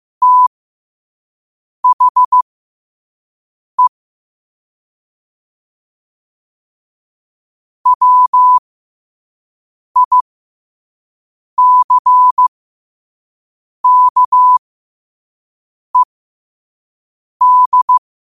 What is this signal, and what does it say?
Morse code, heard as a keyed tone, about 15 wpm, spaced as for 5 wpm, 1000 Hz. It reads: THE WICKED
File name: qotdfiles / 5wpm